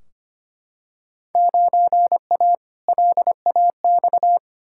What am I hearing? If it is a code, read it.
Morse code, heard as a keyed tone, about 25 wpm, 700 Hz. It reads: 9A LAX
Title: Field Day Exchanges Encore - 25wpm